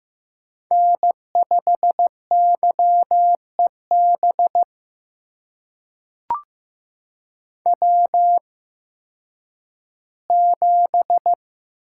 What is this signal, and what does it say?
Morse code, heard as a keyed tone, about 15 wpm, spaced as for 10 wpm, 700 Hz. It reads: N5YEB W7
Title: USA Call Signs - 5 Characters - 8x Character Spacing - 15wpm